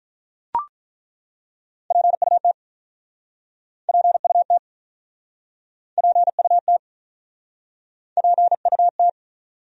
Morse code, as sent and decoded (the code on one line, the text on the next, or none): .--. ..- - / .--. ..- - / .--. ..- - / .--. ..- -
PUT PUT PUT PUT